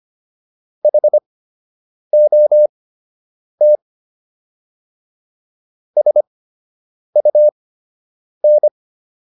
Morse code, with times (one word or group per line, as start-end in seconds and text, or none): 0.8-3.8 HOT
6.0-8.7 SUN